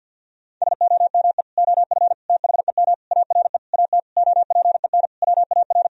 Morse code, heard as a keyed tone, about 46 wpm, 700 Hz. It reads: SOME OF THEM ARE AT OPEN WAR
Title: Sentences from Top 1000 Words - Review Part 2 - 50wpm